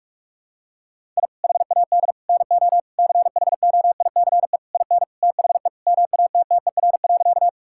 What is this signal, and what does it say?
Morse code, heard as a keyed tone, about 43 wpm, 700 Hz. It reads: I HAD NO CHOICE IN THE MATTER.
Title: Sentences from Top 1000 Words - Review Part 6 - 45wpm